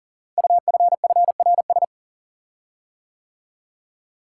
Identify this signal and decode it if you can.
Morse code, heard as a keyed tone, about 38 wpm, 700 Hz. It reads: UFFRS
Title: Code Groups - 5 Letters - 40wpm